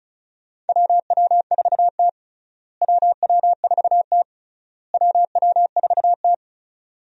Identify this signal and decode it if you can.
Morse code, heard as a keyed tone, about 34 wpm, 700 Hz. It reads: WW4T WW4T WW4T
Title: USA Call Signs - 4 Characters - Repeated 3x - 35wpm